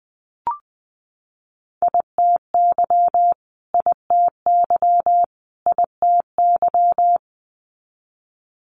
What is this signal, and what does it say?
Morse code, heard as a keyed tone, about 19 wpm, 700 Hz. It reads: ITY ITY ITY
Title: Multi-letter Phonemes Repeated 3x 20wpm